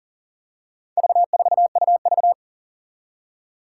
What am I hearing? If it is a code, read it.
Morse code, heard as a keyed tone, about 40 wpm, 700 Hz. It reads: V4UV
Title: Four Challenging Letter-Number Combinations 40wpm